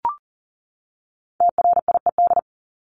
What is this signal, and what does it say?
Morse code, heard as a keyed tone, about 40 wpm, 700 Hz. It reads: TRIED
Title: ED Suffix - Top 100 Words with Base Words 40wpm